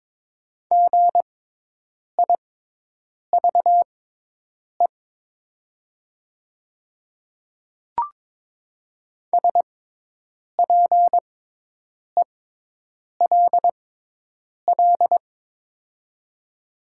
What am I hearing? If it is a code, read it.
Morse code, heard as a keyed tone, about 22 wpm, spaced as for 8 wpm, 700 Hz. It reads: GIVE SPELL
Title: 6x Character Spacing - Top 300 Words - 22wpm